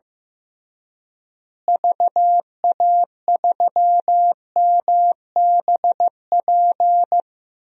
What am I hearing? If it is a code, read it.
Morse code, heard as a keyed tone, about 15 wpm, 700 Hz. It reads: VA3MBP
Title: DXCC-100 Call Signs - 6 Characters - Repeated 2x - 15wpm